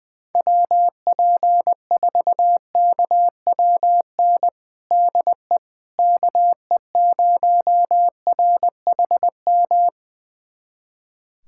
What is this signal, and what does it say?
Morse code, heard as a keyed tone, about 21 wpm, 700 Hz. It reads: WP4KWN DE KE0RHM
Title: USA Callsign DE Callsign - 6 Characters - 20wpm